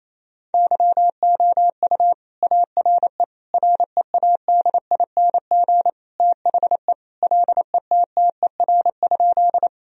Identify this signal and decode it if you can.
Morse code, heard as a keyed tone, about 28 wpm, 700 Hz. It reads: YOU ARE READING THE LETTER?
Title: Sentences from Top 1000 Words - Review Part 3 - 28wpm